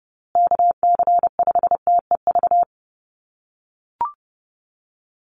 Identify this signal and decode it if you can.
Morse code, heard as a keyed tone, about 29 wpm, 700 Hz. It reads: KC5TEV
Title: USA Call Signs Repeated 2x 30wpm